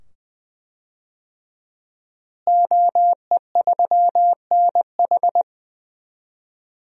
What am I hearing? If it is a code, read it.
Morse code, heard as a keyed tone, about 20 wpm, 700 Hz. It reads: OE3NH
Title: DXCC-100 Call Signs - 5 Characters - Repeated 2x - 20wpm